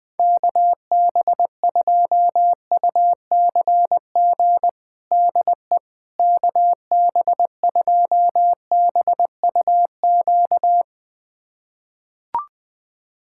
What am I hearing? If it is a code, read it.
Morse code, heard as a keyed tone, about 20 wpm, 700 Hz. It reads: KB2UCG DE KB2BUQ
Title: USA Callsign DE Callsign - 6 Characters - 20wpm